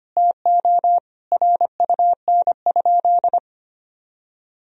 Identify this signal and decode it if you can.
Morse code, heard as a keyed tone, about 25 wpm, 700 Hz. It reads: TO RUN?
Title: Sentences from Top 1000 Words - Review Part 2 - 25wpm